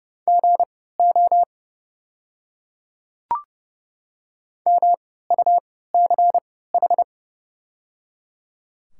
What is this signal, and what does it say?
Morse code, heard as a keyed tone, about 30 wpm, spaced as for 17 wpm, 690 Hz. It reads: GO MUCH
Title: 3x Character Spacing - Top 200 Words - 30wpm